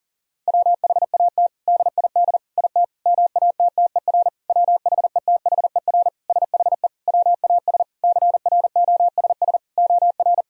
Morse code, heard as a keyed tone, about 37 wpm, 700 Hz. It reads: WHAT DID IT MATTER WHETHER SHE WAS CROSS OR
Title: Sentences from Top 1000 Words - Review Part 3 - 40wpm